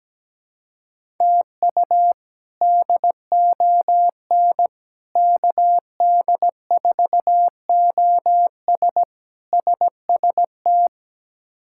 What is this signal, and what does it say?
Morse code, heard as a keyed tone, about 17 wpm, 700 Hz. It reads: TU DON KD4OS SST
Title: K1USN SST QSO Practice - 17wpm